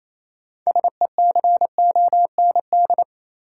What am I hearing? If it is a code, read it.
Morse code, heard as a keyed tone, about 28 wpm, 700 Hz. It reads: SECOND
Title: Sentences from Top 300 Words - New Word Drill - 28wpm